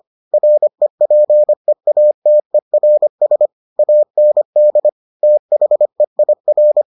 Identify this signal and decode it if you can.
Morse code, heard as a keyed tone, about 25 wpm, 600 Hz. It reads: REPEATERS AND THEIR